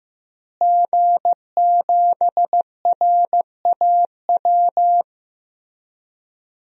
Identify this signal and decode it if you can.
Morse code, heard as a keyed tone, about 16 wpm, 700 Hz. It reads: G7RAW
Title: DXCC-100 Call Signs - 5 Characters - Repeated 3x - 15wpm